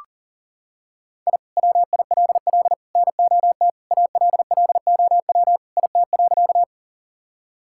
Morse code, heard as a keyed tone, about 37 wpm, 700 Hz. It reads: I WILL NOT ALLOW IT.
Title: Sentences from Top 1000 Words - Review Part 5 - 40wpm